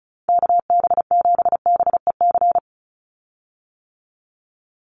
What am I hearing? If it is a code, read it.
Morse code, heard as a keyed tone, about 34 wpm, 700 Hz. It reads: KB7BEC
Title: USA Call Signs 35wpm